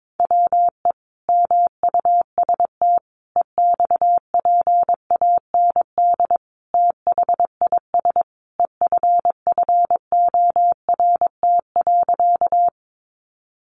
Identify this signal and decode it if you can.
Morse code, heard as a keyed tone, about 22 wpm, 700 Hz. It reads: WE MUST EXPAND THIS EFFORT.
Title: Sentences from Top 1400 Words 22wpm